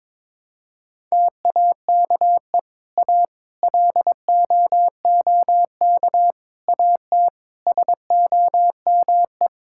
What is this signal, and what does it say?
Morse code, heard as a keyed tone, about 22 wpm, 700 Hz. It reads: TAKE A LOOK AT SOME
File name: Sentences with 7-Word Phrases - 22wpm